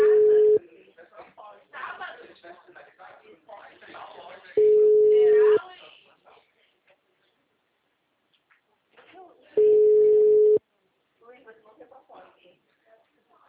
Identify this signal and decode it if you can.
Ringback tone (Europe, 425 Hz), with speech.
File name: out- -2009- - - .31746